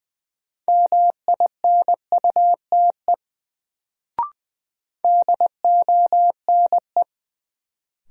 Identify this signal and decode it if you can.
Morse code, heard as a keyed tone, about 21 wpm, 700 Hz. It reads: MINUTE DONE
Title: Sentences from Top 500 Words - New Word Drill - 20wpm